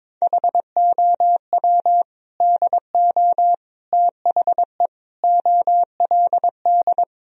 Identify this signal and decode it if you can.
Morse code, heard as a keyed tone, about 22 wpm, 700 Hz. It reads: HOW DO THE OLD
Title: Sentences from Top 1000 Words - Review Part 1 - 22wpm